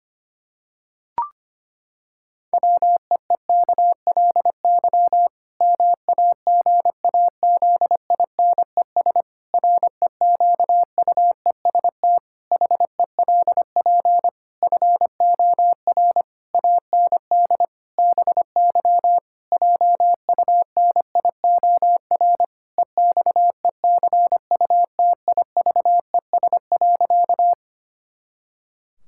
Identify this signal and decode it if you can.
Morse code, heard as a keyed tone, about 25 wpm, 700 Hz. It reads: WEEKLY MAGAZINES REQUEST HELP FOR AND BY JUNIOR EXECUTIVES.
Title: Pangram Challenge v2 - 25wpm